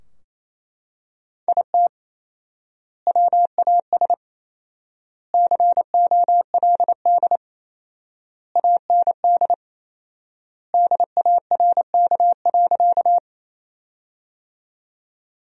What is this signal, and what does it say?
Morse code, heard as a keyed tone, about 28 wpm, 700 Hz. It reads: IT WAS COLD AND DARK.
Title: 4x Word-Spacing - Sentences from Top 500 Words - 28wpm